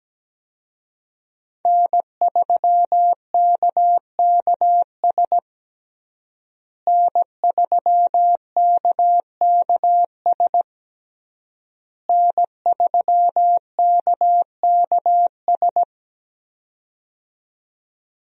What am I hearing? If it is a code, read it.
Morse code, heard as a keyed tone, about 17 wpm, 700 Hz. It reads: N3KKS N3KKS N3KKS
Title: USA Call Signs - 5 Characters - Repeated 3x - 17wpm